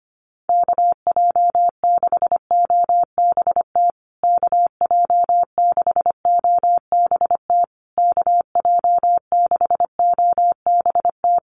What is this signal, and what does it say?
Morse code, heard as a keyed tone, about 25 wpm, 700 Hz. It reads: KJ6OBT KJ6OBT KJ6OBT
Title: USA Call Signs Repeated 3x 25wpm